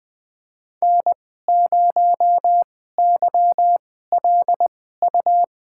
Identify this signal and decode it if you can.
Morse code, heard as a keyed tone, about 21 wpm, spaced as for 15 wpm, 700 Hz. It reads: N0YLU
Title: USA Call Signs - 5 Characters - 2x Character Spacing - 20wpm